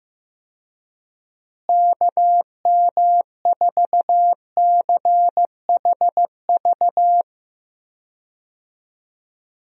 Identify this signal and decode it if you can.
Morse code, heard as a keyed tone, about 15 wpm, 700 Hz. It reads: KM4CHV
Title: USA Call Signs - 6 Characters - Repeated 2x - 15wpm